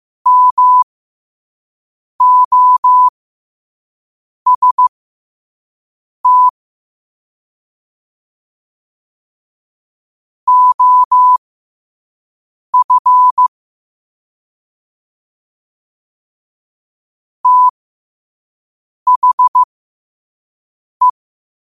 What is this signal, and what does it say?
Morse code, heard as a keyed tone, about 15 wpm, spaced as for 5 wpm, 1000 Hz. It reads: MOST OF THE
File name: qotdfiles / 5wpm